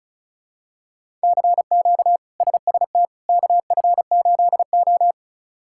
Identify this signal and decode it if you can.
Morse code, heard as a keyed tone, about 36 wpm, 700 Hz. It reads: CQ SST KF8O
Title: K1USN SST QSO Practice - 35wpm